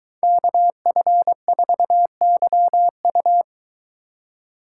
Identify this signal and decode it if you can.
Morse code, heard as a keyed tone, about 22 wpm, 700 Hz. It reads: KF4YU